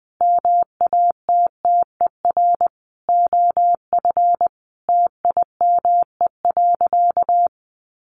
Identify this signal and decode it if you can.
Morse code, heard as a keyed tone, about 20 wpm, 700 Hz. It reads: MATTER OF TIME.
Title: Sentences from Top 1400 Words 20wpm